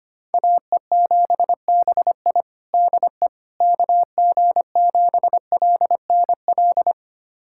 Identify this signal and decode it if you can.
Morse code, heard as a keyed tone, about 25 wpm, 700 Hz. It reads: AE7BI DE KG7LNL
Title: USA Callsign DE Callsign 25wpm